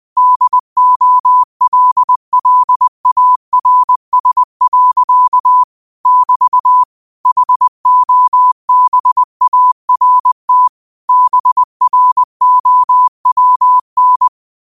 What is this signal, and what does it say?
Morse code, heard as a keyed tone, about 20 wpm, 1000 Hz. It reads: DOLLARS. <BT> HOBART BROWN